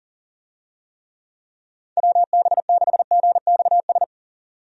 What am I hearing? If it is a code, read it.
Morse code, heard as a keyed tone, about 40 wpm, 700 Hz. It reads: WB6GXS
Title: USA Call Signs - 6 Characters - 40wpm